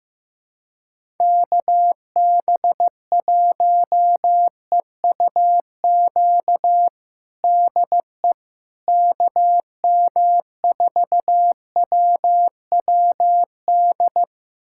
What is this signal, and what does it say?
Morse code, heard as a keyed tone, about 16 wpm, 700 Hz. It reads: KB1EUQ DE KM4WWD
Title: USA Callsign DE Callsign - 6 Characters - 15wpm